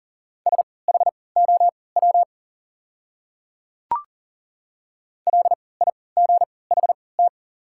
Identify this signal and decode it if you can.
Morse code, heard as a keyed tone, about 40 wpm, spaced as for 23 wpm, 700 Hz. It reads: SHOW LIGHT
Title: 3x Character Spacing - Top 300 Words - 40wpm